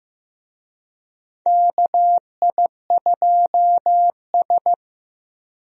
Morse code, heard as a keyed tone, about 15 wpm, 700 Hz. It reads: KI2S
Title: USA Call Signs - 4 Characters - Repeated 2x - 15wpm